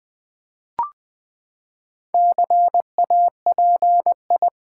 Morse code, heard as a keyed tone, about 20 wpm, 700 Hz. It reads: CAPI
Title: Build A Word - 4 Letters At A Time - 10-Letter Words - 20wpm